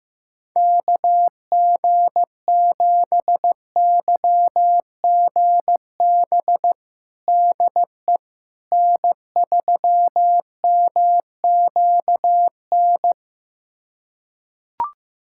Morse code, heard as a keyed tone, about 15 wpm, 700 Hz. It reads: KG7YGB DE N3MQN
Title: USA Callsign DE Callsign 15wpm